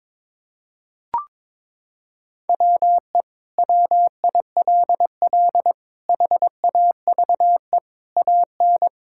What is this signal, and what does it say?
Morse code, heard as a keyed tone, about 22 wpm, 700 Hz. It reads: WE WILL HAVE AN
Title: Sentences from Top 1000 Words - Review Part 5 - 22wpm